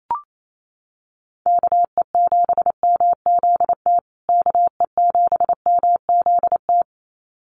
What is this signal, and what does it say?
Morse code, heard as a keyed tone, about 28 wpm, 700 Hz. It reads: KE7MZT KE7MZT
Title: USA Call Signs Repeated 2x 28wpm